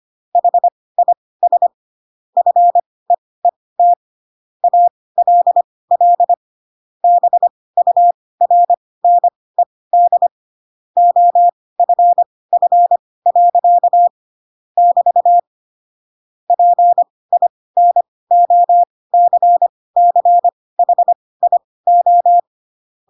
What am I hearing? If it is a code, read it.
Morse code, heard as a keyed tone, about 25 wpm, spaced as for 18 wpm, 700 Hz. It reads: HIS FEET ALL BURNED OFF. <BT> PINOCCHIO
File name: Pinocchio / Pinocchio ch6 12 25 ik1hge